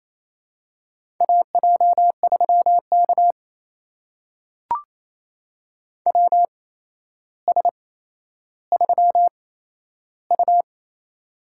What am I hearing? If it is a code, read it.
Morse code, heard as a keyed tone, about 27 wpm, spaced as for 12 wpm, 700 Hz. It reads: AJ3K WS3U
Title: USA Call Signs - 4 Characters - 8x Character Spacing - 28wpm